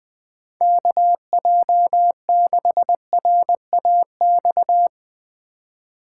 Morse code, heard as a keyed tone, about 20 wpm, 700 Hz. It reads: KJ6RAX
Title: USA Call Signs - 6 Characters - 4x Character Spacing - 20wpm